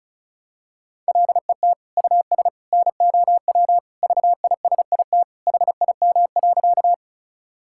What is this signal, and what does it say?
Morse code, heard as a keyed tone, about 34 wpm, 700 Hz. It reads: LET US NOW VISIT HIM.
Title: Sentences from Top 1000 Words - Review Part 6 - 35wpm